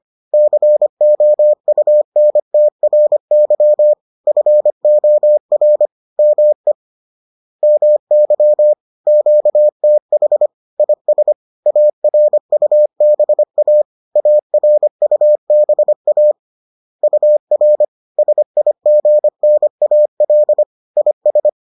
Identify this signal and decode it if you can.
Morse code, heard as a keyed tone, about 25 wpm, 600 Hz. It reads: COUNTRY FOR ME MY QTH IS ARUBA ARUBA UR SIGNAL IS